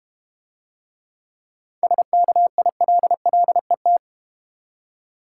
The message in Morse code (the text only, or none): SKILLET